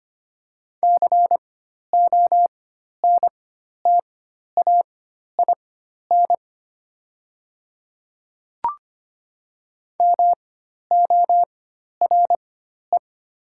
Morse code, heard as a keyed tone, about 25 wpm, spaced as for 12 wpm, 700 Hz. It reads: CONTAIN MORE
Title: 4x Character Spacing - Top 500 Words - 25wpm